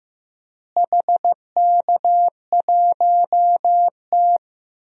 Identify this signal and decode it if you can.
Morse code, heard as a keyed tone, about 15 wpm, 700 Hz. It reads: HK1T
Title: DXCC-100 Call Signs - 4 Characters - Repeated 3x - 15wpm